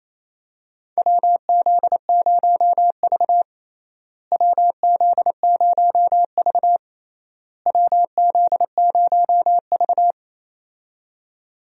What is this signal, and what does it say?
Morse code, heard as a keyed tone, about 28 wpm, 700 Hz. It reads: WZ0V WZ0V WZ0V
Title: USA Call Signs - 4 Characters - Repeated 3x - 28wpm